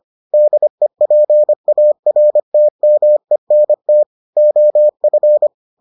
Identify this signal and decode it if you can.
Morse code, heard as a keyed tone, about 25 wpm, 600 Hz. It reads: DEPARTMENT OF